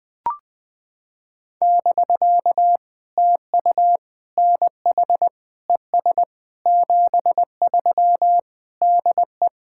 Morse code, heard as a keyed tone, about 20 wpm, 700 Hz. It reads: <BK> TU NH ES 73 DE
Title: POTA QSO Practice - 20wpm